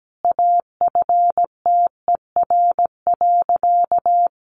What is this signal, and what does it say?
Morse code, heard as a keyed tone, about 17 wpm, 700 Hz. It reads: AFTER.
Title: Sentences from Top 1500 Words 17wpm